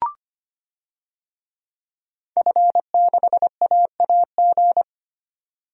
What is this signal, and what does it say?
Morse code, heard as a keyed tone, about 25 wpm, 700 Hz. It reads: F6AAG
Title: DXCC-100 Call Signs - 5 Characters - 25wpm